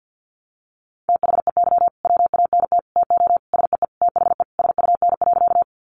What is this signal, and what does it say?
Morse code, heard as a keyed tone, about 50 wpm, 700 Hz. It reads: THEY WANT TO SEE THE SUN.
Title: Easy Phrases 50wpm